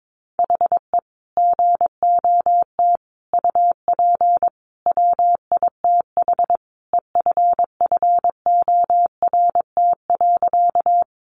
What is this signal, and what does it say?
Morse code, heard as a keyed tone, about 22 wpm, 700 Hz. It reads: HE GOT UP WITH EFFORT.
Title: Sentences from Top 1200 Words 22wpm